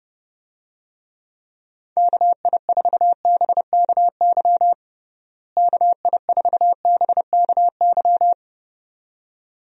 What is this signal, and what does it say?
Morse code, heard as a keyed tone, about 29 wpm, 700 Hz. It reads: KI4BKY KI4BKY
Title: USA Call Signs - 6 Characters - Repeated 2x - 30wpm